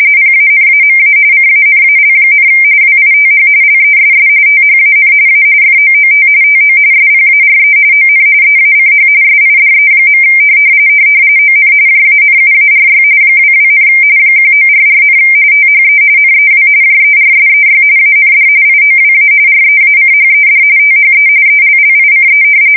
RTTY, about 45.45 baud. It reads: PANAMA CANAL, WITH LESS THAN A FOOT TO SPARE ON EACH SIDE. ON JANUARY 22, 1944, THE NEW JERSEY AND IOWA REACHED THE ELLICE
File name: fsk170 / battleshipnj